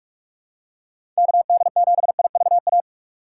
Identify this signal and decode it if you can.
Morse code, heard as a keyed tone, about 47 wpm, 700 Hz. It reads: KD7IVA
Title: USA Call Signs - 2x Character Spacing - 45wpm